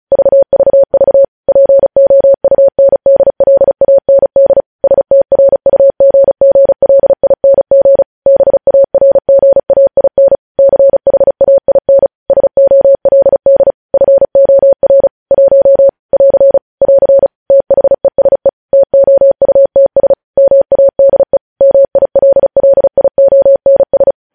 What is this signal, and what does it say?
Morse code, heard as a keyed tone, about 36 wpm, 550 Hz. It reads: VVV POUNDLAND STRUGGLING BARGAIN CHAIN SOLD FOR 1 <AR> <AR> THESE TOUTS MADE MILLIONS
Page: news headlines in morse code at 35 wpm